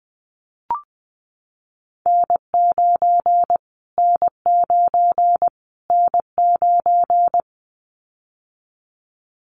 Morse code, heard as a keyed tone, about 20 wpm, 700 Hz. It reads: N9 N9 N9
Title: Callsign Prefixes for Top 25 DXCC - Repeated 3x - 20wpm